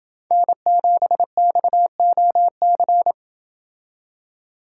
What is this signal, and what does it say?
Morse code, heard as a keyed tone, about 26 wpm, 700 Hz. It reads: N7XOC